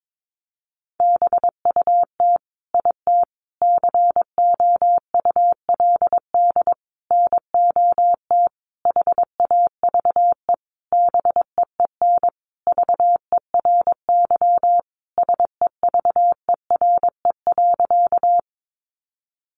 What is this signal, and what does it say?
Morse code, heard as a keyed tone, about 22 wpm, 700 Hz. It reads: BUT IT COULD NOT HAVE BEEN VERY SEVERE.
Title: Sentences from Top 1400 Words 22wpm